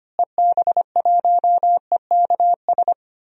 Morse code, heard as a keyed tone, about 25 wpm, 700 Hz. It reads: EB1EKS
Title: DXCC-100 Call Signs - 6 Characters - Repeated 2x - 25wpm